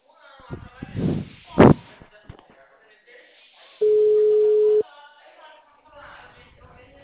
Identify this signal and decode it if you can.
Ringback tone (Europe, 425 Hz), with speech.
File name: out- -2011- - - .1392